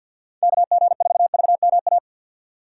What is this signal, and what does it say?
Morse code, heard as a keyed tone, about 48 wpm, 700 Hz. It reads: KG4VMA
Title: USA Call Signs - 6 Characters - 3x Character Spacing - 50wpm